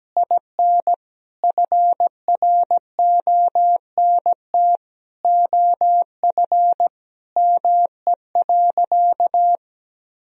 Morse code, heard as a keyed tone, about 17 wpm, 700 Hz. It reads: IN FRONT OF ME.
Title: Sentences from Top 1000 Words - Review Part 2 - 17wpm